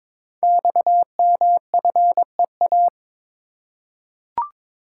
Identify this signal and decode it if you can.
Morse code, heard as a keyed tone, about 22 wpm, 700 Hz. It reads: XMFEA
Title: Code Groups - 5 Letter:Numbers - 22wpm